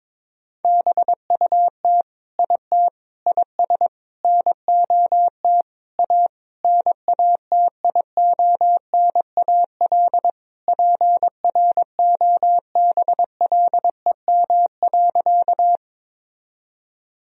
Morse code, heard as a keyed tone, about 21 wpm, 700 Hz. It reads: BUT IT IS NOT A NATIONAL PROBLEM.
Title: Sentences from Top 1000 Words - Review Part 4 - 22wpm